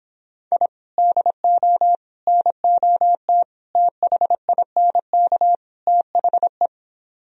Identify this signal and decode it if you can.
Morse code, heard as a keyed tone, about 26 wpm, 700 Hz. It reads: I DO NOT THINK THE